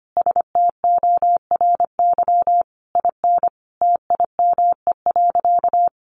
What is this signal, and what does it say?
Morse code, heard as a keyed tone, about 25 wpm, 700 Hz. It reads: STORY IN TIME.
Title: Sentences from Top 1400 Words 25wpm